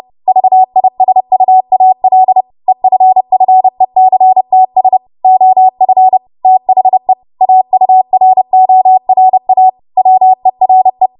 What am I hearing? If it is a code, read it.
Morse code, heard as a keyed tone, about 30 wpm, 750 Hz. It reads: VISUAL EFFECTS OF THE AURORA WERE
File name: Morse / 30WPM